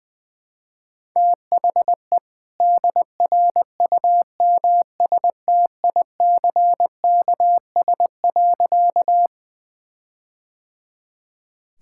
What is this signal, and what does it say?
Morse code, heard as a keyed tone, about 21 wpm, 700 Hz. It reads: THE DRUMSTICKS.
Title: All-Age Appropriate Jokes - 20wpm